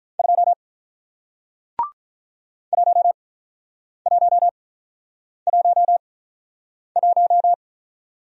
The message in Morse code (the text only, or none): 2 1 1 1 1